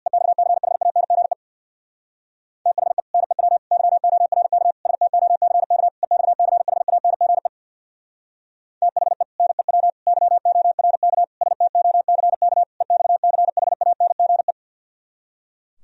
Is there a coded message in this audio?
Yes: Morse code from 0.1 to 14.5 s, heard as a keyed tone, about 65 wpm, 690 Hz. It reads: EXCHANGE THE NEW YORK STOCK EXCHANGE THE NEW YORK STOCK EXCHANGE